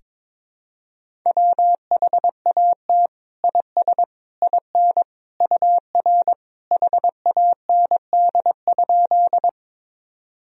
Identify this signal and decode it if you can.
Morse code, heard as a keyed tone, about 21 wpm, 700 Hz. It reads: WHAT IS IN UR HAND?
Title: Sentences from Top 200 Words and Abbreviations 22wpm